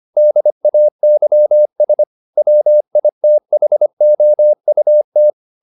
Morse code, heard as a keyed tone, about 25 wpm, 600 Hz. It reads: DAYS WITHOUT
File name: the old man and the sea part 1 25wpm